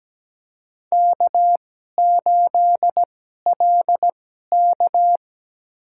Morse code, heard as a keyed tone, about 17 wpm, spaced as for 12 wpm, 700 Hz. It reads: K8LK
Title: USA Call Signs - 4 Characters - 2x Character Spacing - 17wpm